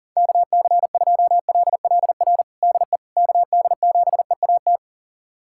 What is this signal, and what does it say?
Morse code, heard as a keyed tone, about 40 wpm, 690 Hz. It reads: KC2LLR DE KD7EAT
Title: USA Callsign DE Callsign - 6 Characters - 40wpm